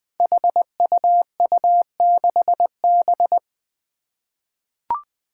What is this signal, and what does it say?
Morse code, heard as a keyed tone, about 20 wpm, 700 Hz. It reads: HUU6B
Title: Code Groups - 5 Letter:Numbers - 20wpm